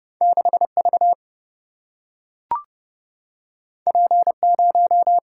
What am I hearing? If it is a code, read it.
Morse code, heard as a keyed tone, about 32 wpm, 700 Hz. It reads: 6 V P 0
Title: Single-Letter Single-Number 30wpm